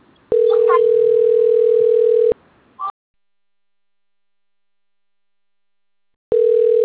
Ringback tone (North America), 440 + 480 Hz.